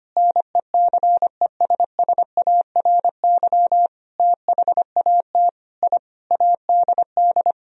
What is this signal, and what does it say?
Morse code, heard as a keyed tone, about 25 wpm, 700 Hz. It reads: NECESSARY THAT I ADD
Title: Sentences from Top 1000 Words - Review Part 1 - 25wpm